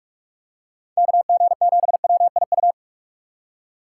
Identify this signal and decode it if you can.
Morse code, heard as a keyed tone, about 43 wpm, 700 Hz. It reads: KG7WIU
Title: USA Call Signs - 6 Characters - 2x Character Spacing - 45wpm